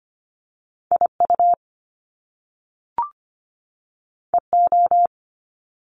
Morse code, heard as a keyed tone, about 25 wpm, 700 Hz. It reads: IU EO
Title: Top Two Letter Combinations in Words 25wpm